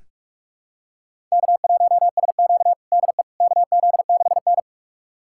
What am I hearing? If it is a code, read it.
Morse code, heard as a keyed tone, about 46 wpm, 700 Hz. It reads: K1SQ DE KZ6N
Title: USA Callsign DE Callsign - 4 Characters - 45wpm